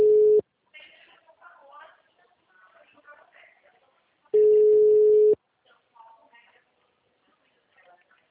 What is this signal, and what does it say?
Ringback tone (Europe, 425 Hz), with speech.